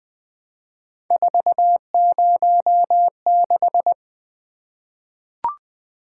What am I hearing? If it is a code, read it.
Morse code, heard as a keyed tone, about 20 wpm, 700 Hz. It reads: 406
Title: Three Letter-Number Combinations - 20wpm